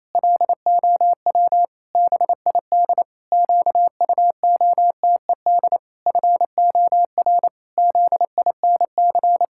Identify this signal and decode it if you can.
Morse code, heard as a keyed tone, about 28 wpm, 700 Hz. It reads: LOW BID QUOTED FOR ZINC
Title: Pangram Challenge v2 - 28wpm